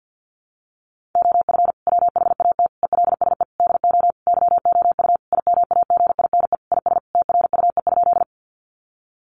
Morse code, heard as a keyed tone, about 50 wpm, 700 Hz. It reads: OF WHAT ELSE DO YOU IMAGINE IS TRUE?
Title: Sentences from Top 1400 Words 50wpm